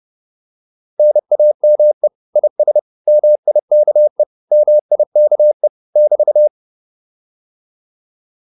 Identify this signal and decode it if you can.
Morse code, heard as a keyed tone, about 30 wpm, 600 Hz. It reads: NAME IS MIKE MIKE <BT>
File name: QSO310 30